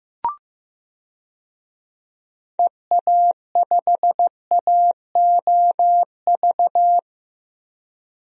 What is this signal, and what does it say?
Morse code, heard as a keyed tone, about 15 wpm, 700 Hz. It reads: EA5AOV
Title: DXCC-100 Call Signs - 6 Characters - 15wpm